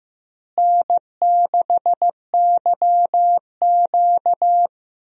Morse code, heard as a keyed tone, about 15 wpm, 700 Hz. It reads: N6YQ
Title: USA Call Signs - 4 Characters - Repeated 3x - 15wpm